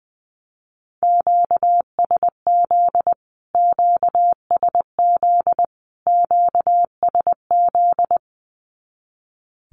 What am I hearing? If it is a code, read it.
Morse code, heard as a keyed tone, about 20 wpm, 700 Hz. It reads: QSZ QSZ QSZ
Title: Q-Codes Repeated 3x 20wpm